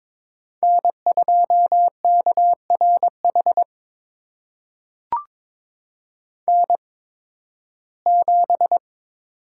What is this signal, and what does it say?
Morse code, heard as a keyed tone, about 22 wpm, spaced as for 14 wpm, 700 Hz. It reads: N2KRH N7
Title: USA Call Signs - 5 Characters - 8x Character Spacing - 22wpm